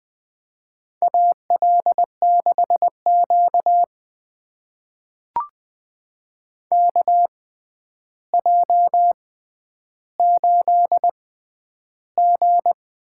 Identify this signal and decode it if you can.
Morse code, heard as a keyed tone, about 20 wpm, spaced as for 10 wpm, 700 Hz. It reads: AL6Q KJ8G
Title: USA Call Signs - 4 Characters - 6x Character Spacing - 20wpm